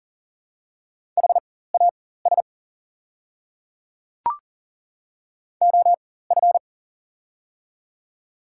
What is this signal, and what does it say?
Morse code, heard as a keyed tone, about 39 wpm, spaced as for 18 wpm, 700 Hz. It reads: HAS OF
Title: 4x Character Spacing - Top 300 Words - 40wpm